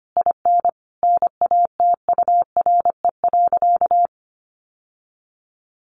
Morse code, heard as a keyed tone, about 25 wpm, 700 Hz. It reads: IN NATURE.
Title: Sentences from Top 1200 Words 25wpm